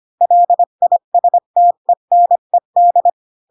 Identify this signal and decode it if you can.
Morse code, heard as a keyed tone, about 25 wpm, 700 Hz. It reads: LISTENED
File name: Pinocchio / Pinocchio ch1 20 25 ik1hge